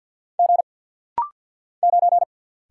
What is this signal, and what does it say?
Morse code, heard as a keyed tone, about 48 wpm, 700 Hz. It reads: G 9
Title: Single Letter-Number Alt - Rapid-Fire 50wpm